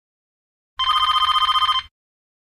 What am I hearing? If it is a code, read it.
There is no code.